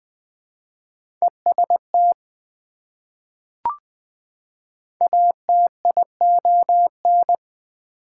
Morse code, heard as a keyed tone, about 20 wpm, 700 Hz. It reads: EST ATION
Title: Prefixes and Suffixes 20wpm